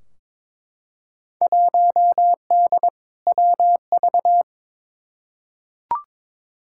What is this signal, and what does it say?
Morse code, heard as a keyed tone, about 22 wpm, 700 Hz. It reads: 1D WV
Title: Field Day Exchanges - 22wpm